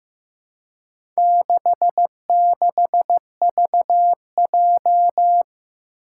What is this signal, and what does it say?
Morse code, heard as a keyed tone, about 15 wpm, 700 Hz. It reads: 66VJ
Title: Four Challenging Letter-Number Combinations 15wpm